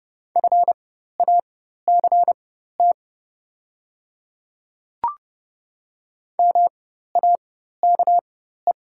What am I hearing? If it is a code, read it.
Morse code, heard as a keyed tone, about 31 wpm, spaced as for 14 wpm, 700 Hz. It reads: FACT MAKE
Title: 4x Character Spacing - Top 500 Words - 30wpm